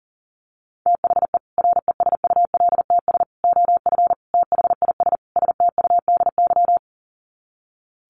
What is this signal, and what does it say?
Morse code, heard as a keyed tone, about 40 wpm, 700 Hz. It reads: THE RESULTS OF THIS STUDY
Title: Sets of 5 Words 40wpm